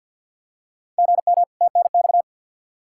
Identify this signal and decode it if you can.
Morse code, heard as a keyed tone, about 51 wpm, 700 Hz. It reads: GM TNX
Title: K1USN SST QSO Practice 50wpm